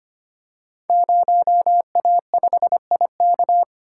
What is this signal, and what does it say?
Morse code, heard as a keyed tone, about 25 wpm, 700 Hz. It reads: 0A5IK
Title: Code Groups - 5 Letter:Numbers - 25wpm